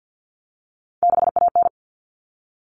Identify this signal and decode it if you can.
Morse code, heard as a keyed tone, about 50 wpm, 700 Hz. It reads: 6AN
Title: Common Cut Numbers - 3-Digit 50wpm